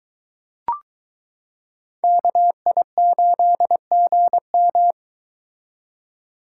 Morse code, heard as a keyed tone, about 22 wpm, 700 Hz. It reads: KI8GM